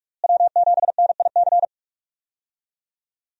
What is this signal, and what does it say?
Morse code, heard as a keyed tone, about 41 wpm, 700 Hz. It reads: W7NIC
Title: USA Call Signs - 5 Characters - Repeated 3x - 45wpm